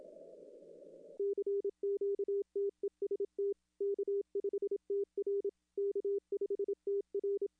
Morse code, heard as a keyed tone, about 27 wpm, 400 Hz. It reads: CQ TEST K5TR K5TR
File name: contests / arrl-160-k5tr-op-w5tn -0600